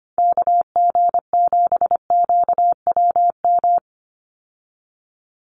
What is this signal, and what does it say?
Morse code, heard as a keyed tone, about 25 wpm, 700 Hz. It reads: KG7QWM
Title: USA Call Signs 25wpm